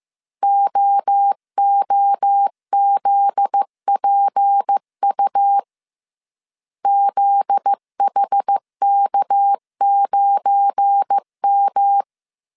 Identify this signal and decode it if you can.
Morse code, heard as a keyed tone, about 15 wpm, 800 Hz. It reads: OOZPU ZHK9M
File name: CW / DET / 1 9 1500C S A 14wpm 3222